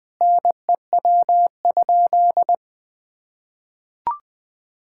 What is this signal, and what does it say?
Morse code, heard as a keyed tone, about 20 wpm, 700 Hz. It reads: NEW?
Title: Sentences with 6-Word Phrases - 20wpm